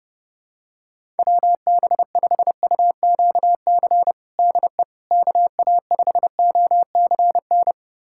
Morse code, heard as a keyed tone, about 30 wpm, 700 Hz. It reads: WB5UQC DE KA5OCN
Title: USA Callsign DE Callsign - 6 Characters - 30wpm